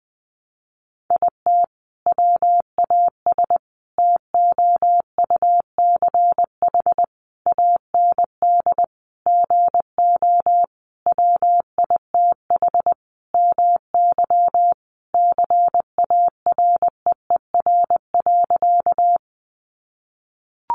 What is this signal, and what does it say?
Morse code, heard as a keyed tone, about 20 wpm, 700 Hz. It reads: IT WAS TOUCH AND GO WITH MY CAREER.
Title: Sentences from Top 1200 Words 20wpm